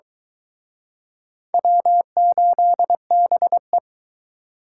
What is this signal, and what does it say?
Morse code, heard as a keyed tone, about 24 wpm, 700 Hz. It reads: W8BE